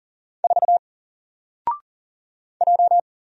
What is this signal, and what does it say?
Morse code, heard as a keyed tone, about 39 wpm, 700 Hz. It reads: 4 J
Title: Single Letter-Number 40wpm